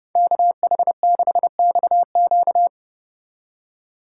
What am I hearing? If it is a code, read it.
Morse code, heard as a keyed tone, about 29 wpm, 700 Hz. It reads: KH6XQ